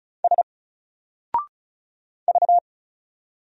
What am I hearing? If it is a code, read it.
Morse code, heard as a keyed tone, about 34 wpm, 700 Hz. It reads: S V
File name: Single-Letter-Number-35wpm